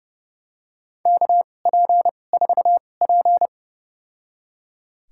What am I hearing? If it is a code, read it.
Morse code, heard as a keyed tone, about 30 wpm, spaced as for 22 wpm, 700 Hz. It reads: KP4P
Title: USA Call Signs - 4 Characters - 2x Character Spacing - 30wpm